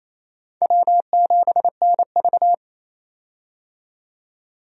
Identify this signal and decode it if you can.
Morse code, heard as a keyed tone, about 28 wpm, 690 Hz. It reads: W7NV
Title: USA Call Signs - 4 Characters - 28wpm